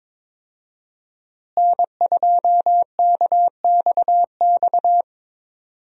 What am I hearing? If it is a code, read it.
Morse code, heard as a keyed tone, about 22 wpm, 700 Hz. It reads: N2KXX
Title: USA Call Signs - 5 Characters - Repeated 3x - 22wpm